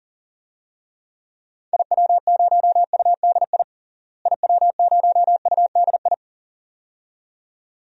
Morse code, heard as a keyed tone, about 40 wpm, 700 Hz. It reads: IW0UDI IW0UDI
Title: DXCC-100 Call Signs - 6 Characters - Repeated 2x - 40wpm